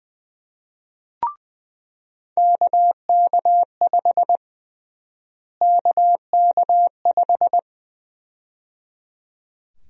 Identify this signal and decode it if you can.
Morse code, heard as a keyed tone, about 20 wpm, 700 Hz. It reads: KK5 KK5
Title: USA Call Sign Prefixes - Repeated 2x - 20wpm